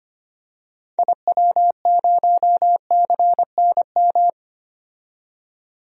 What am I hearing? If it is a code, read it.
Morse code, heard as a keyed tone, about 25 wpm, 700 Hz. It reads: IW0CNM
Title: DXCC-100 Call Signs - 6 Characters - 25wpm